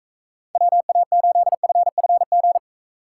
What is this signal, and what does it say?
Morse code, heard as a keyed tone, about 42 wpm, 700 Hz. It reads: WA8FFG